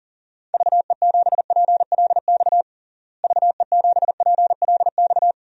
Morse code, heard as a keyed tone, about 40 wpm, 700 Hz. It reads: VE7PLX VE7PLX
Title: DXCC-100 Call Signs - 6 Characters - Repeated 2x - 40wpm